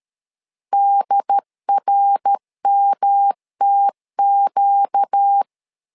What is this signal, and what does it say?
Morse code, heard as a keyed tone, about 13 wpm, 800 Hz. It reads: DRMTQ
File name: LEZIONI / 4 Q1 12wpm 2028